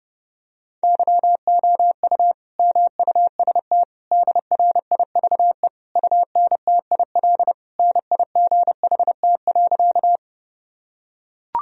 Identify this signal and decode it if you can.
Morse code, heard as a keyed tone, about 30 wpm, 700 Hz. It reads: YOU MUST DRIVE UNTIL NIGHT.
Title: Sentences from Top 1000 Words - Review Part 2 - 30wpm